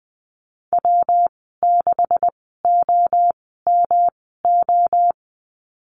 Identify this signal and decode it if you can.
Morse code, heard as a keyed tone, about 20 wpm, spaced as for 15 wpm, 700 Hz. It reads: W6OMO
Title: 161 Lesson 26g - Letter g - Call Signs 20wpm